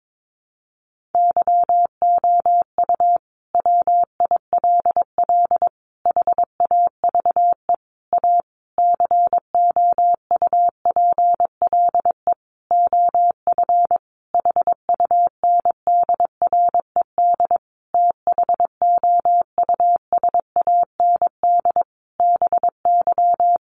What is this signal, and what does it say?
Morse code, heard as a keyed tone, about 22 wpm, 700 Hz. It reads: YOU WILL HAVE A COUPLE OF HUNDRED THOUSAND BY
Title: Sentences from Top 1100 Words 22wpm